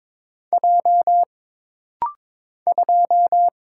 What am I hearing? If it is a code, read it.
Morse code, heard as a keyed tone, about 22 wpm, 700 Hz. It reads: J 2
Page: This rapid-fire timing is geared to push your ICR (Instant Character Recognition) skills.